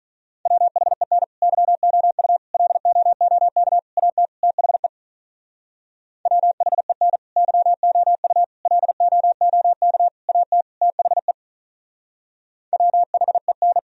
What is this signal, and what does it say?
Morse code, heard as a keyed tone, about 46 wpm, 700 Hz. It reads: WHEN YOU LOOK AT THE WHEN YOU LOOK AT THE WHEN